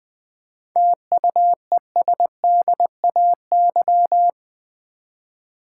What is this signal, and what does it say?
Morse code, heard as a keyed tone, about 20 wpm, 700 Hz. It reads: TUESDAY
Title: 7 Letter Words - First 3 Letters Given - 20wpm